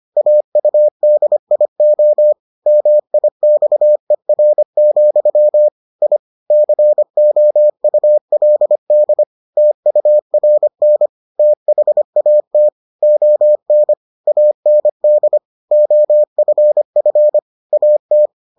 Morse code, heard as a keyed tone, about 25 wpm, 600 Hz. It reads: AUDIO MIXER, I COULD TURN THAT ON AND OFF AT